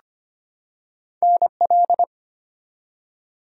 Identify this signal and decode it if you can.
Morse code, heard as a keyed tone, about 25 wpm, 700 Hz. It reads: NL
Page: morse-code-ninja-practice-25wpm